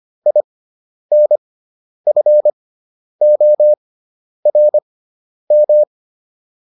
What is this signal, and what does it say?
Morse code, heard as a keyed tone, about 25 wpm, spaced as for 10 wpm, 600 Hz. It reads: INFORM